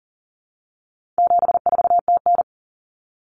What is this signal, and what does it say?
Morse code, heard as a keyed tone, about 38 wpm, 700 Hz. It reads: 74TN
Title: Common Cut Numbers - 4-Digit 40wpm